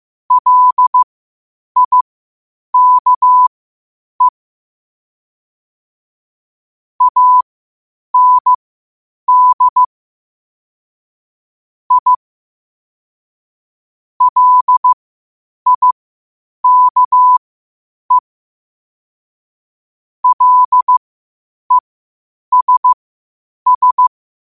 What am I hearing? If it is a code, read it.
Morse code, heard as a keyed tone, about 15 wpm, spaced as for 8 wpm, 1000 Hz. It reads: LIKE AND I LIKE LESS
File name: qotdfiles / 8wpm